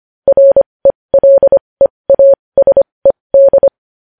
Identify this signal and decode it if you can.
Morse code, heard as a keyed tone, about 25 wpm, spaced as for 20 wpm, 550 Hz. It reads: RELEASED
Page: news headlines in morse code at 20 wpm